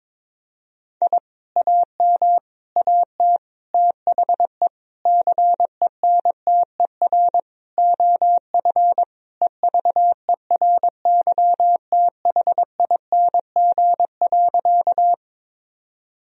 Morse code, heard as a keyed tone, about 22 wpm, 700 Hz. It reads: I AM AT THE CENTER OF EVERYTHING.
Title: Sentences from Top 1000 Words - Review Part 2 - 22wpm